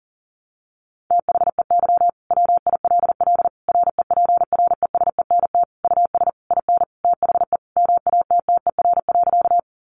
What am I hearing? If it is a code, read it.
Morse code, heard as a keyed tone, about 40 wpm, 700 Hz. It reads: THEY WILL REPRESENT US IN THE MATTER.
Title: Sentences from Top 1200 Words 40wpm